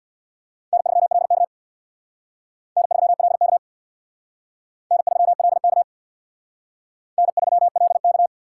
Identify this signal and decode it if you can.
Morse code, heard as a keyed tone, about 64 wpm, 700 Hz. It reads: N3LK N3LK N3LK N3LK